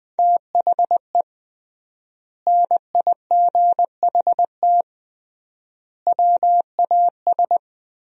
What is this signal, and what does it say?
Morse code, heard as a keyed tone, about 20 wpm, 700 Hz. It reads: THE NIGHT WAS
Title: 3x Word-Spacing - Sentences from Top 400 Words - 20wpm